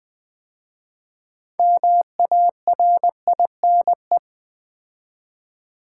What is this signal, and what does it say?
Morse code, heard as a keyed tone, about 20 wpm, 700 Hz. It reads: MARINE
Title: 6 Letter Words - First 3 Letters Given - 20wpm